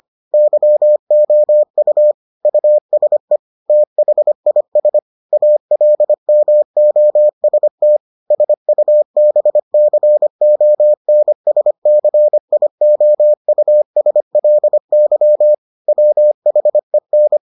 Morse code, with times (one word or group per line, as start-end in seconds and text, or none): 0.3-2.1 YOU
2.5-3.4 USE
3.7-5.0 THIS
5.3-8.0 ALMOST
8.3-15.5 SUBCONSCIOUSLY
15.9-17.4 WHEN